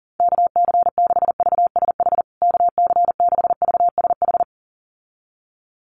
Morse code, heard as a keyed tone, about 40 wpm, 700 Hz. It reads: KC6VSH KC6VSH
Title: USA Call Signs Repeated 2x 40wpm